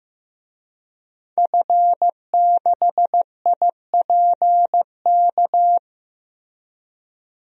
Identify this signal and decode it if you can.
Morse code, heard as a keyed tone, about 15 wpm, 700 Hz. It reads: F6IPK
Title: DXCC-100 Call Signs - 5 Characters - Repeated 2x - 15wpm